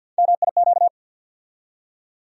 Morse code, heard as a keyed tone, about 47 wpm, 700 Hz. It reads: MIQ
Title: USA Call Sign Postfixes - 3 Letters 50wpm